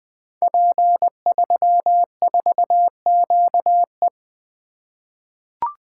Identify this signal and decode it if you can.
Morse code, heard as a keyed tone, about 20 wpm, 700 Hz. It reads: P34QE
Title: Code Groups - 5 Letter:Numbers - 20wpm